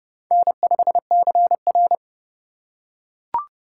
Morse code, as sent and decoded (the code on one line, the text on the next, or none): -. ..... -.-. .-.
N5CR